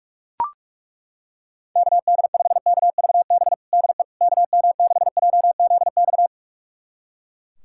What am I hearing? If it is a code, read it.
Morse code, heard as a keyed tone, about 41 wpm, 700 Hz. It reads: KD5KVB DE KM6JZX
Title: USA Callsign DE Callsign - 6 Characters - 45wpm